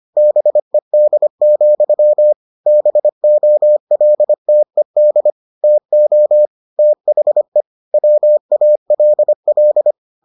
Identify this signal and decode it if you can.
Morse code, heard as a keyed tone, about 25 wpm, 600 Hz. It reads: BED, BOLTED TO THE WALL